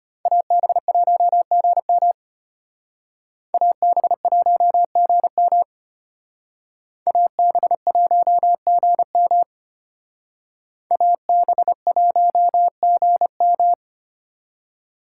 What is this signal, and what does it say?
Morse code, heard as a keyed tone, about 32 wpm, 700 Hz. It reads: AB1GM AB1GM AB1GM AB1GM